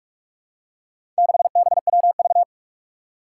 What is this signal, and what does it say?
Morse code, heard as a keyed tone, about 46 wpm, 700 Hz. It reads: 6BWV
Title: Four Challenging Letter-Number Combinations 45wpm